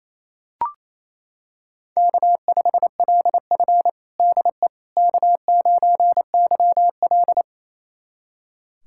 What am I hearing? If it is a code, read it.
Morse code, heard as a keyed tone, about 28 wpm, 700 Hz. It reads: K5LF DE K9YL
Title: USA Callsign DE Callsign - 4 Characters - 28wpm